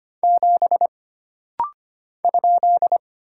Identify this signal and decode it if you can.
Morse code, heard as a keyed tone, about 25 wpm, 700 Hz. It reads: 7 ?
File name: Single Character (Letter:Number:Punctuation:Prosign) - Rapid-Fire - 25wpm